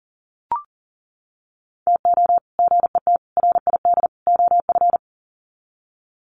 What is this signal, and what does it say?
Morse code, heard as a keyed tone, about 40 wpm, 700 Hz. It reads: TO GET RID OF
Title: Sets of 4 Words 40wpm